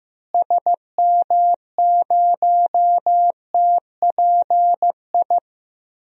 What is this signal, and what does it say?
Morse code, heard as a keyed tone, about 16 wpm, 700 Hz. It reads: SM0TPI
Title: DXCC-100 Call Signs - 6 Characters - Repeated 2x - 15wpm